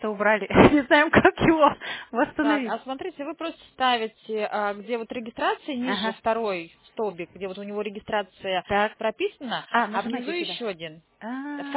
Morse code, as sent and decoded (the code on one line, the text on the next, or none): none